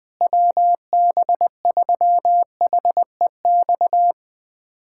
Morse code, heard as a keyed tone, about 21 wpm, 700 Hz. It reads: WB3HEX
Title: USA Callsign DE Callsign - 6 Characters - 20wpm